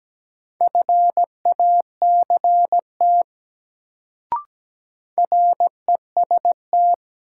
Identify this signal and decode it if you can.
Morse code, heard as a keyed tone, about 17 wpm, 700 Hz. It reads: FACT REST
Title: Sentences from Top 500 Words - New Word Drill - 17wpm